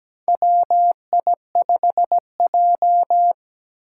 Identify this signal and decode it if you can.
Morse code, heard as a keyed tone, about 17 wpm, 700 Hz. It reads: WI5J
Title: Four Challenging Letter-Number Combinations 17wpm